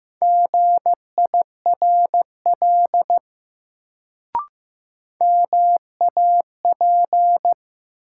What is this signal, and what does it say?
Morse code, heard as a keyed tone, about 15 wpm, 700 Hz. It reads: GIRL MAP
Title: Sentences from Top 400 Words - New Word Drill - 15wpm